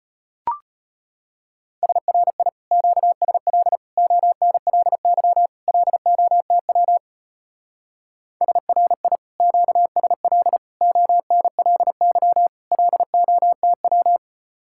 Morse code, heard as a keyed tone, about 33 wpm, 700 Hz. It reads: SRI QSL ONLY LOTW SRI QSL ONLY LOTW